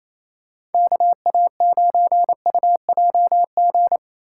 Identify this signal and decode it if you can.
Morse code, heard as a keyed tone, about 28 wpm, 700 Hz. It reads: KA9UJG
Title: USA Call Signs - 6 Characters - 4x Character Spacing - 28wpm